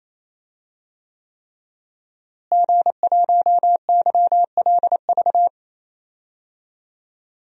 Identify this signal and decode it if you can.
Morse code, heard as a keyed tone, about 27 wpm, 700 Hz. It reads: G1YLV
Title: DXCC-100 Call Signs - 5 Characters - Repeated 3x - 28wpm